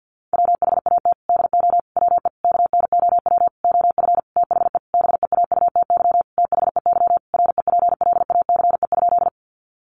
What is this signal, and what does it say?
Morse code, heard as a keyed tone, about 50 wpm, 700 Hz. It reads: WHAT DO WE KNOW OF THE BEAUTY THEY REPLACE?
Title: Sentences from Top 1400 Words 50wpm